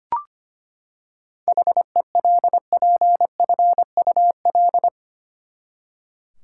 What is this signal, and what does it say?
Morse code, heard as a keyed tone, about 25 wpm, 700 Hz. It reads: HELPFUL
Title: Seven Letter Words - 25wpm